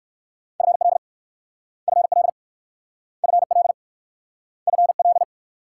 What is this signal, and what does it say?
Morse code, heard as a keyed tone, about 60 wpm, 700 Hz. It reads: FL FL FL FL